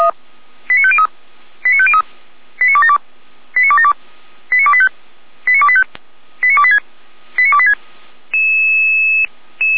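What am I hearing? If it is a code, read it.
Five-tone selective call (ZVEI), heard as quick runs of five tones, tones of about 70 ms. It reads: 89503 89503 89272 89272 89285 89285 89286 89286
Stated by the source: ZVEI five-tone call received by SDR, from the audio archive of the Feuerwehr Orlamünde fire brigade.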